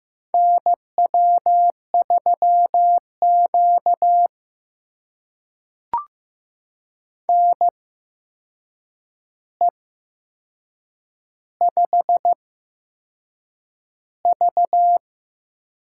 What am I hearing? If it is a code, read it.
Morse code, heard as a keyed tone, about 15 wpm, spaced as for 6 wpm, 700 Hz. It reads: NW3Q NE5V
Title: USA Call Signs - 4 Characters - 8x Character Spacing - 15wpm